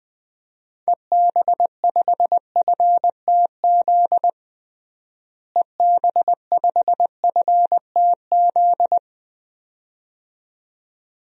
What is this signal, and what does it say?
Morse code, heard as a keyed tone, about 20 wpm, 700 Hz. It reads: EB5FTZ EB5FTZ
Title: DXCC-100 Call Signs - 6 Characters - Repeated 2x - 20wpm